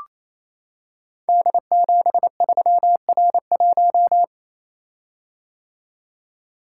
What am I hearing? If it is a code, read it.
Morse code, heard as a keyed tone, about 27 wpm, 700 Hz. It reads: D73R1
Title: Code Groups - 5 Letter:Numbers - 28wpm